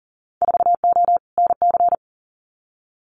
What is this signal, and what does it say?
Morse code, heard as a keyed tone, about 40 wpm, 700 Hz. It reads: 4O NC
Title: Winter Field Day Exchanges 40wpm